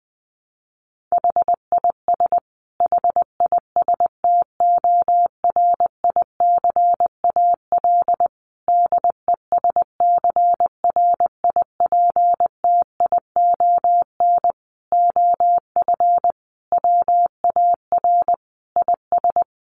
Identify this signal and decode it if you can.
Morse code, heard as a keyed tone, about 19 wpm, 700 Hz. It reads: HIS HISTORICAL DESCRIPTION OF WAR IS
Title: Sentences from Top 1300 Words 20wpm